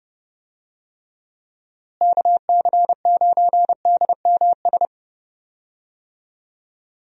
Morse code, heard as a keyed tone, about 30 wpm, 700 Hz. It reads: KC9DMS
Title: USA Call Signs - 6 Characters - Repeated 3x - 30wpm